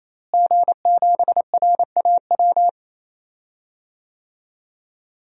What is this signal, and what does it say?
Morse code, heard as a keyed tone, about 27 wpm, 700 Hz. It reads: G7RAW
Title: DXCC-100 Call Signs - 5 Characters - Repeated 3x - 28wpm